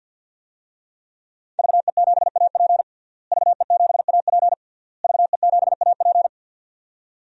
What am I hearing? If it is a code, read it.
Morse code, heard as a keyed tone, about 46 wpm, 700 Hz. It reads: VE7AP VE7AP VE7AP
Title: DXCC-100 Call Signs - 5 Characters - Repeated 3x - 50wpm